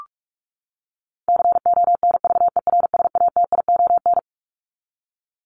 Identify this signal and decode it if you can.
Morse code, heard as a keyed tone, about 44 wpm, 700 Hz. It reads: CONVERSATION
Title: Top 1751-2000 Words 45wpm